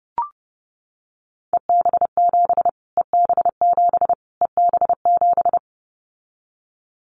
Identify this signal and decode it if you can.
Morse code, heard as a keyed tone, about 30 wpm, 700 Hz. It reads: EB7 EB7 EB7
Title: Callsign Prefixes for Top 25 DXCC - Repeated 3x - 30wpm